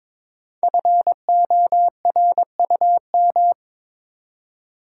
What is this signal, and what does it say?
Morse code, heard as a keyed tone, about 23 wpm, 700 Hz. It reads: FORUM
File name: 5 Letter Words - First 3 Letters Given - 22wpm